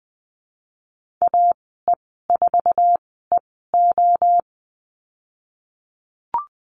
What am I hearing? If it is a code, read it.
Morse code, heard as a keyed tone, about 20 wpm, spaced as for 14 wpm, 700 Hz. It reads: AE4EO
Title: 048 Lesson 9g - Number 4 - Call Signs 20wpm